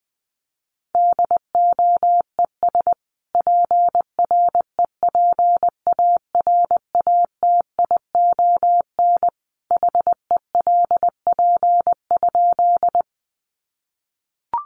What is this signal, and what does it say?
Morse code, heard as a keyed tone, about 20 wpm, 700 Hz. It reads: DOES PREPARATION HELP?
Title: Sentences from Top 1400 Words 20wpm